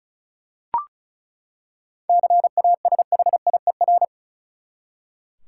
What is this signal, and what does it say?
Morse code, heard as a keyed tone, about 36 wpm, 700 Hz. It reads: CASHIER
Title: Practice QSO - Exchange 3 - More Info - 35wpm